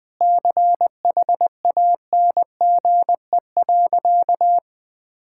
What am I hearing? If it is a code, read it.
Morse code, heard as a keyed tone, about 20 wpm, 700 Hz. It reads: CHANGE.
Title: Sentences from Top 1000 Words - Review Part 6 - 20wpm